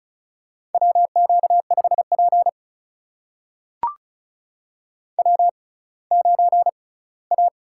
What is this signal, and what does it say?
Morse code, heard as a keyed tone, about 36 wpm, spaced as for 20 wpm, 700 Hz. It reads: WQ5P W9A
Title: USA Call Signs - 4 Characters - 6x Character Spacing - 35wpm